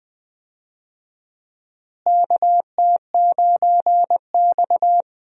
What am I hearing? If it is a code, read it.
Morse code, heard as a keyed tone, about 20 wpm, 700 Hz. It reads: KT9X
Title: USA Call Signs - 4 Characters - Repeated 2x - 20wpm